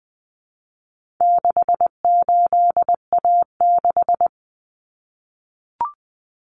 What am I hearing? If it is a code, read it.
Morse code, heard as a keyed tone, about 20 wpm, 700 Hz. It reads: 68A6
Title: Common Cut Numbers - 4-Digit 20wpm